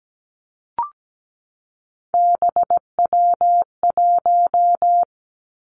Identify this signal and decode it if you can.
Morse code, heard as a keyed tone, about 17 wpm, 700 Hz. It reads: BW1
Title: Three Challenging Letter-Number Combinations 17wpm